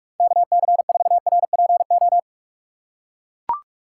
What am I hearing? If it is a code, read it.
Morse code, heard as a keyed tone, about 44 wpm, 700 Hz. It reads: KC4RPO